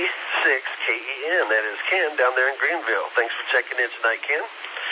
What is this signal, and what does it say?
Emergency Alert System (SAME) tones, heard as harsh buzzing bursts, with speech. It is heard from 0.6 to 4.4 s.